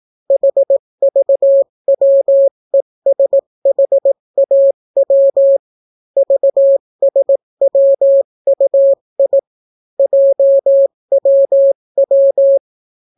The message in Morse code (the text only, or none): HVWESHAW VSWUI JWW